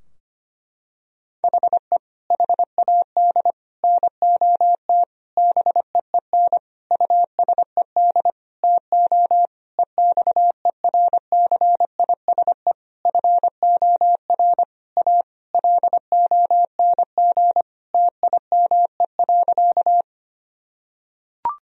Morse code, heard as a keyed tone, about 25 wpm, 700 Hz. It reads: HE HAD NOT BEEN USED TO EXERCISE FOR A LONG TIME.
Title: Sentences from Top 1000 Words - Review Part 4 - 25wpm